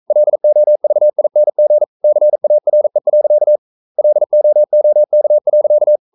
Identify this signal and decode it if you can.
Morse code, heard as a keyed tone, about 42 wpm, 600 Hz. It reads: LOVING CARE. LOOK.
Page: Romanen Never af Ken Follet på engelsk.